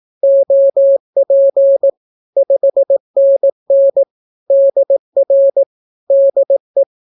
Morse code, heard as a keyed tone, about 18 wpm, 550 Hz. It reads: OP 5NN DR DE
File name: naukapodstaw / skroty / skroty